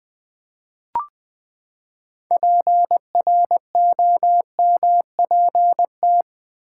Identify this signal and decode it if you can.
Morse code, heard as a keyed tone, about 20 wpm, 700 Hz. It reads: PROMPT
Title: Six Letter Words - 20wpm